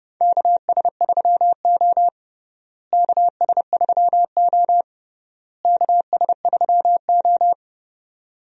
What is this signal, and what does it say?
Morse code, heard as a keyed tone, about 29 wpm, 700 Hz. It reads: KS3O KS3O KS3O
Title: USA Call Signs - 4 Characters - Repeated 3x - 30wpm